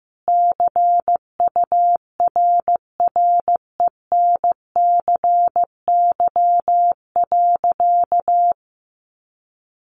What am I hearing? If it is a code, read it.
Morse code, heard as a keyed tone, about 15 wpm, 700 Hz. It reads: CURRENCY.
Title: Sentences from Top 1500 Words 15wpm